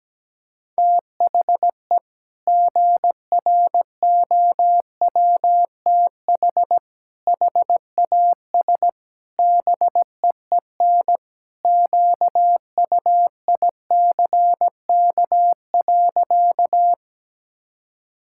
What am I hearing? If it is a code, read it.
Morse code, heard as a keyed tone, about 17 wpm, 700 Hz. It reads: THE GROWTH HAS BEEN QUICK.
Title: Sentences from Top 1000 Words - Review Part 5 - 17wpm